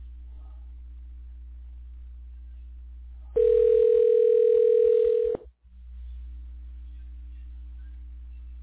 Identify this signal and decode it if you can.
Ringback tone (North America), 440 + 480 Hz.